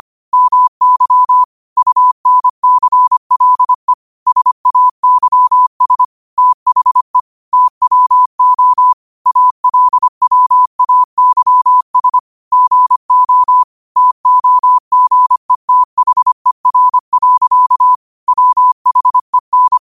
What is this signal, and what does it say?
Morse code, heard as a keyed tone, about 25 wpm, 1000 Hz. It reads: MY UNCLE SAYS THE TWO ALWAYS GO TOGETHER. WHEN
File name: qotdfiles / 25wpm